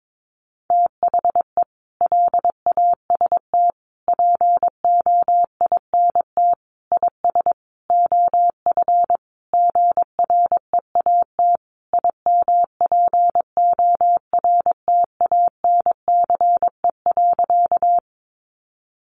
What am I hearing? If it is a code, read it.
Morse code, heard as a keyed tone, about 22 wpm, 700 Hz. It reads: THE LAST POINT IS OF GREAT IMPORTANCE.
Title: Sentences from Top 1200 Words 22wpm